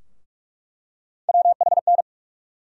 Morse code, heard as a keyed tone, about 46 wpm, 700 Hz. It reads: WHN
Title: USA Call Sign Postfixes - 3 Letters 45wpm